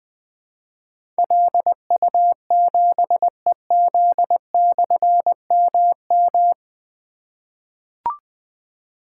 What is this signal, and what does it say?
Morse code, heard as a keyed tone, about 20 wpm, 700 Hz. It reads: LU7EZ/MM
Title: Difficult Call Signs 20wpm